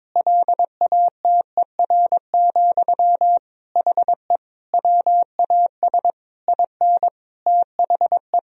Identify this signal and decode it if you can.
Morse code, heard as a keyed tone, about 22 wpm, 700 Hz. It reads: LATER, HE WAS IN THE
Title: Sentences from Top 1000 Words - Review Part 4 - 22wpm